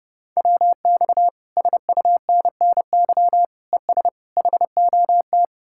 Morse code, heard as a keyed tone, about 38 wpm, 700 Hz. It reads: WX SUNNY ES HOT